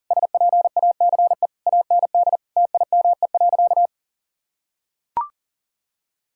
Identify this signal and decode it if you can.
Morse code, heard as a keyed tone, about 40 wpm, 700 Hz. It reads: SPACE AND TIME.
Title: Sentences from Top 1000 Words - Review Part 5 - 40wpm